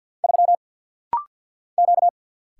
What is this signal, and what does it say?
Morse code, heard as a keyed tone, about 50 wpm, 700 Hz. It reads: 3 Q
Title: Single Letter-Number - Rapid-Fire 50wpm